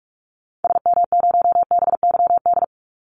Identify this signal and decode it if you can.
Morse code, heard as a keyed tone, about 46 wpm, 700 Hz. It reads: SM0BYD
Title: Call Signs - 45wpm